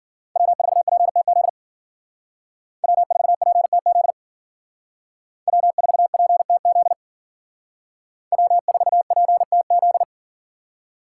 Morse code, heard as a keyed tone, about 50 wpm, 700 Hz. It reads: W4PTZ W4PTZ W4PTZ W4PTZ